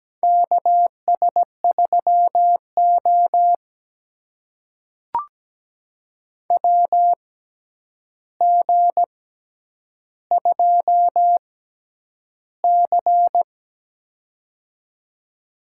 Morse code, heard as a keyed tone, about 17 wpm, spaced as for 9 wpm, 700 Hz. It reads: KS3O WG2C
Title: USA Call Signs - 4 Characters - 6x Character Spacing - 17wpm